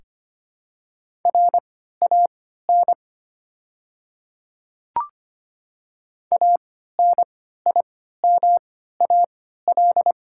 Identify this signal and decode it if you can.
Morse code, heard as a keyed tone, about 25 wpm, spaced as for 14 wpm, 690 Hz. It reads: RAN ANIMAL
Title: 3x Character Spacing - Top 500 Words - 25wpm